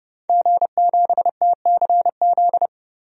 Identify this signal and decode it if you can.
Morse code, heard as a keyed tone, about 32 wpm, 700 Hz. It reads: G7TCZ
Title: DXCC-100 Call Signs - 5 Characters - 30wpm